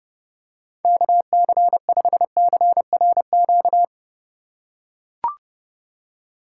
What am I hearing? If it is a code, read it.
Morse code, heard as a keyed tone, about 32 wpm, 700 Hz. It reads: KC5CRQ
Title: USA Call Signs - 6 Characters - 4x Character Spacing - 30wpm